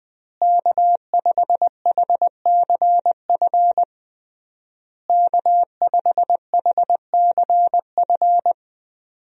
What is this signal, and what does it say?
Morse code, heard as a keyed tone, about 21 wpm, 700 Hz. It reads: K5HCF K5HCF
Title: USA Call Signs - 5 Characters - Repeated 3x - 20wpm